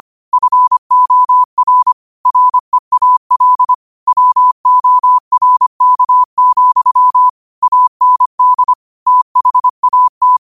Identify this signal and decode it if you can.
Morse code, heard as a keyed tone, about 25 wpm, 1000 Hz. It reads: FOR REAL WORK, AND THAT
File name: qotdfiles / 25wpm